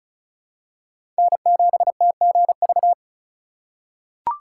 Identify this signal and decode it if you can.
Morse code, heard as a keyed tone, about 34 wpm, 700 Hz. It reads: N7TGV
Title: USA Call Signs - 2x Character Spacing - 35wpm